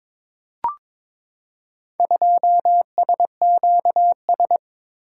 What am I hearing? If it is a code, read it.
Morse code, heard as a keyed tone, about 22 wpm, 700 Hz. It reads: 2SQS
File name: practice / 22wpm / Four Letter-Number Combinations - 22wpm